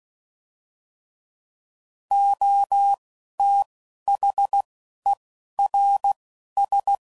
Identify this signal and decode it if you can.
Morse code, heard as a keyed tone, about 16 wpm, spaced as for 12 wpm, 800 Hz. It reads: OTHERS
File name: Prefixes / By First Letter / Os